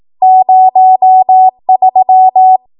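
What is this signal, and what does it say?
Morse code, heard as a keyed tone, about 18 wpm, 750 Hz. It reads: 03
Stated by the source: 18 WPM Morse code practice from a national amateur radio association's code archive.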